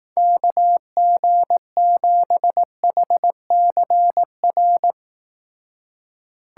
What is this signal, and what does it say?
Morse code, heard as a keyed tone, about 18 wpm, 700 Hz. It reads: KG7HCR
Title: Speed Racing - USA 2x3 Call Signs 15wpm